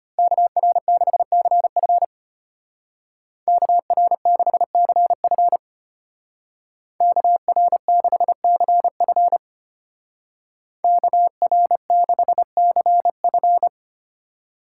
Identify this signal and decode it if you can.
Morse code, heard as a keyed tone, about 30 wpm, 700 Hz. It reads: KR6CF KR6CF KR6CF KR6CF
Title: Speed Racing - USA Call Signs 25wpm